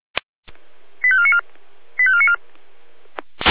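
Five-tone selective call (ZVEI), heard as quick runs of five tones, tones of about 70 ms. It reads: 85494 85494
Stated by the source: ZVEI five-tone selective call received by SDR.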